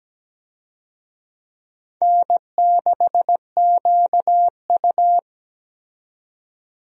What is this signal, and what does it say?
Morse code, heard as a keyed tone, about 17 wpm, 700 Hz. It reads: N6QU
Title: USA Call Signs - 4 Characters - 17wpm